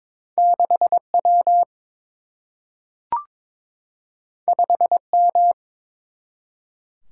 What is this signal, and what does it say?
Morse code, heard as a keyed tone, about 22 wpm, 700 Hz. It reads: 6W 5M
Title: Single-Letter Single-Number 22wpm